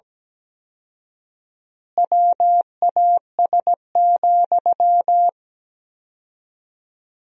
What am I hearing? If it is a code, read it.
Morse code, heard as a keyed tone, about 17 wpm, 700 Hz. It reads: WAS,
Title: 4x Word-Spacing - Sentences from Top 400 Words - 17wpm